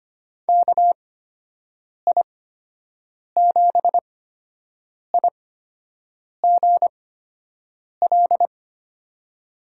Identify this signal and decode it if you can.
Morse code, heard as a keyed tone, about 25 wpm, spaced as for 7 wpm, 700 Hz. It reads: KI7IGL
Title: USA Call Signs - 8x Character Spacing - 25wpm